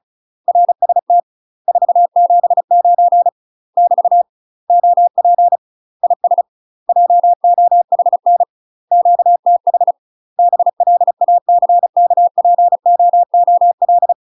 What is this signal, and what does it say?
Morse code, heard as a keyed tone, about 35 wpm, 700 Hz. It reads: RST 479 <BT> OP IS JOHN QTH BLACKPOOL
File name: UK MASTER QSOs TRAINING / 35WPM / UK MASTER QSOs 0038